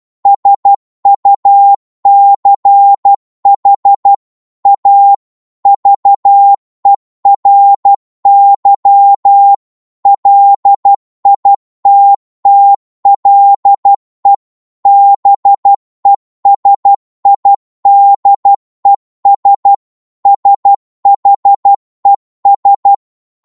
Morse code, heard as a keyed tone, about 12 wpm, 800 Hz. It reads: SUCH A VERY LITTLE BESIDES SHES
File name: 12wpm